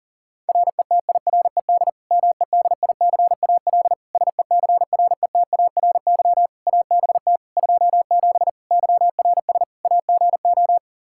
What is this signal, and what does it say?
Morse code, heard as a keyed tone, about 37 wpm, 700 Hz. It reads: RETIRED MEDICAL SECRETARY ABT 27 YRS AGO
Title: Practice QSO - Exchange 3 - More Info - 40wpm